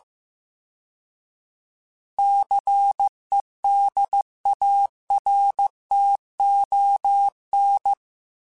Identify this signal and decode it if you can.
Morse code, heard as a keyed tone, about 15 wpm, 800 Hz. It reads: CEDARTON